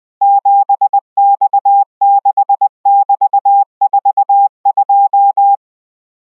Morse code, heard as a keyed tone, about 20 wpm, 800 Hz. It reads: 7X6<BT>42